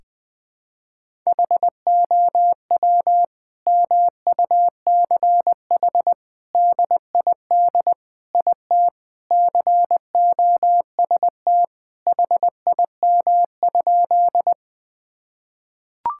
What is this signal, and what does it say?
Morse code, heard as a keyed tone, about 20 wpm, 700 Hz. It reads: HOW MUCH DID IT COST HIM?
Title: Sentences from Top 1000 Words - Review Part 5 - 20wpm